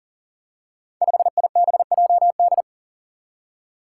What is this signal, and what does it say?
Morse code, heard as a keyed tone, about 38 wpm, 700 Hz. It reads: 5IBJD
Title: Five Challenging Letter-Number Combinations 40wpm